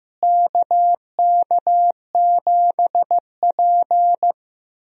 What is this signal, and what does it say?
Morse code, heard as a keyed tone, about 15 wpm, 700 Hz. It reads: KK7P
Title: USA Call Signs - 4 Characters - Repeated 2x - 15wpm